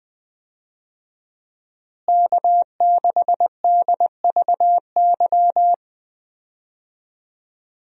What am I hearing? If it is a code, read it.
Morse code, heard as a keyed tone, about 20 wpm, 700 Hz. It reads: K6DVY
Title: USA Call Signs - 5 Characters - Repeated 3x - 20wpm